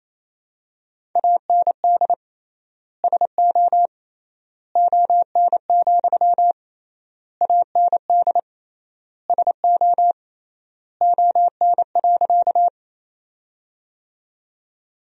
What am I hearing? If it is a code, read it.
Morse code, heard as a keyed tone, about 28 wpm, 700 Hz. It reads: AND SO ON, AND SO ON.
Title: 3x Word-Spacing - Sentences from Top 100 Words - 28wpm